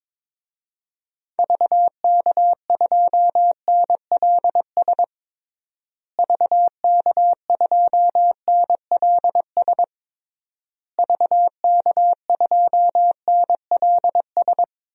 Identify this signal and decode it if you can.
Morse code, heard as a keyed tone, about 22 wpm, 700 Hz. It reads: VK2NLS VK2NLS VK2NLS
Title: DXCC-100 Call Signs - 6 Characters - Repeated 3x - 22wpm